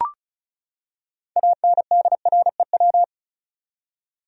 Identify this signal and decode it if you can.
Morse code, heard as a keyed tone, about 36 wpm, 700 Hz. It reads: ANDREW
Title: USA Common Names - 35wpm